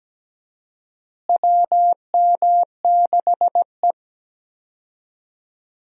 Morse code, heard as a keyed tone, about 17 wpm, 700 Hz. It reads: WM6E
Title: USA Call Signs - 4 Characters - Repeated 2x - 17wpm